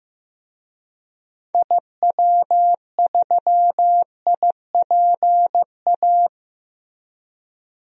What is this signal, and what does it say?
Morse code, heard as a keyed tone, about 15 wpm, 700 Hz. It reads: IW3IPA
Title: DXCC-100 Call Signs - 6 Characters - Repeated 3x - 15wpm